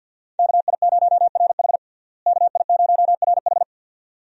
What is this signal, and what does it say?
Morse code, heard as a keyed tone, about 49 wpm, 700 Hz. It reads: KI0RH KI0RH
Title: USA Call Signs - 5 Characters - Repeated 2x - 50wpm